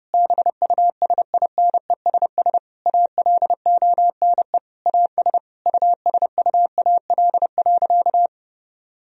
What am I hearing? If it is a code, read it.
Morse code, heard as a keyed tone, about 32 wpm, 700 Hz. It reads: BUSINESS ALONE AS USUAL.
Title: Sentences from Top 1000 Words - Review Part 2 - 30wpm